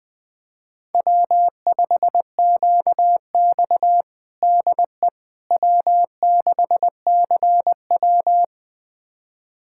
Morse code, heard as a keyed tone, about 21 wpm, 700 Hz. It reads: W5QX DE W6CW
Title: USA Callsign DE Callsign - 4 Characters - 20wpm